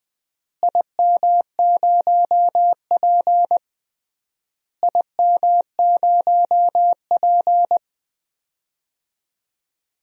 Morse code, heard as a keyed tone, about 20 wpm, 700 Hz. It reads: IM0P IM0P
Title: DXCC-100 Call Signs - 4 Characters - Repeated 3x - 20wpm